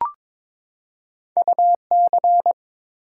Morse code, heard as a keyed tone, about 22 wpm, 700 Hz. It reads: UC
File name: practice / 22wpm / Two Letter-Number Combinations - 22wpm